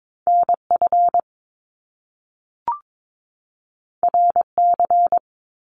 Morse code, heard as a keyed tone, about 22 wpm, 700 Hz. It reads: NF RC
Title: Top Two Letter Combinations in Words 22wpm